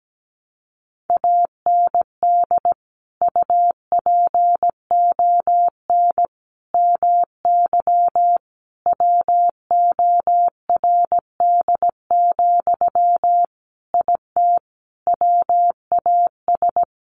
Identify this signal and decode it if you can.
Morse code, heard as a keyed tone, about 17 wpm, 700 Hz. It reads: AND UPON MY WORD, IT WAS
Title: Sentences from Top 1100 Words 17wpm